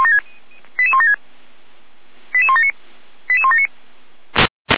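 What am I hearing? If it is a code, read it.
Five-tone selective call (ZVEI), heard as quick runs of five tones, tones of about 70 ms. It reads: 88176 88179 88179
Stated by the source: ZVEI five-tone selective call 88352 received on SDR.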